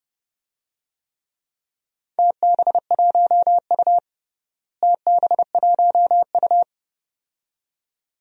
Morse code, heard as a keyed tone, about 30 wpm, 700 Hz. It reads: TB1U TB1U
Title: DXCC-100 Call Signs - 4 Characters - Repeated 2x - 30wpm